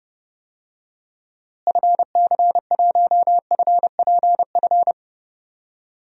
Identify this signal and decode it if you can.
Morse code, heard as a keyed tone, about 32 wpm, 700 Hz. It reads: FC1FPF
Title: DXCC-100 Call Signs - 6 Characters - Repeated 3x - 30wpm